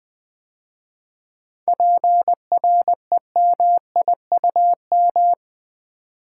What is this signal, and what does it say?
Morse code, heard as a keyed tone, about 20 wpm, 700 Hz. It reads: PREMIUM
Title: 7 Letter Words - First 3 Letters Given - 20wpm